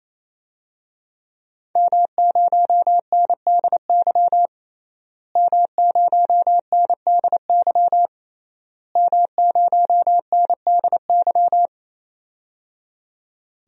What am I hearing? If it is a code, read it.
Morse code, heard as a keyed tone, about 28 wpm, 700 Hz. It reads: M0NDY M0NDY M0NDY
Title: DXCC-100 Call Signs - 5 Characters - Repeated 3x - 28wpm